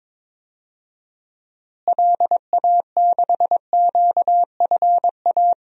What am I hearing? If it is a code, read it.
Morse code, heard as a keyed tone, about 23 wpm, 700 Hz. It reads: LA6QFA
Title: DXCC-100 Call Signs - 6 Characters - Repeated 2x - 22wpm